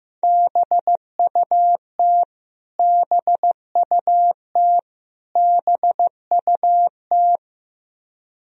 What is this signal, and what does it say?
Morse code, heard as a keyed tone, about 16 wpm, 700 Hz. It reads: BUT BUT BUT
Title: Top 100 Words Repeated 15wpm